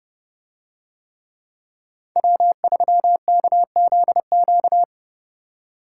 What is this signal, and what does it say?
Morse code, heard as a keyed tone, about 30 wpm, 700 Hz. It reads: W3KZQ